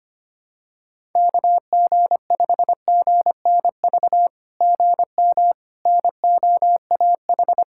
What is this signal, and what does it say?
Morse code, heard as a keyed tone, about 25 wpm, 700 Hz. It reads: KG5GNV GM NOAH
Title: SOTA QSO Practice 25wpm